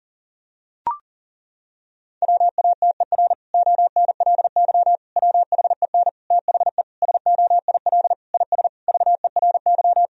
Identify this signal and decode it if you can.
Morse code, heard as a keyed tone, about 40 wpm, 700 Hz. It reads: WATER ONLY WHEN THE SOIL IS VERY
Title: Sentences from Top 1000 Words - Review Part 4 - 40wpm